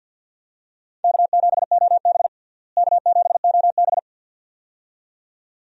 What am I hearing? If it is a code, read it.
Morse code, heard as a keyed tone, about 52 wpm, 690 Hz. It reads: K7OB K7OB
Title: USA Call Signs - 4 Characters - Repeated 2x - 50wpm